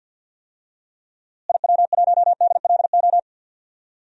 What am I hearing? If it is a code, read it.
Morse code, heard as a keyed tone, about 52 wpm, 700 Hz. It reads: IW1DLO
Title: DXCC-100 Call Signs - 6 Characters - Repeated 2x - 50wpm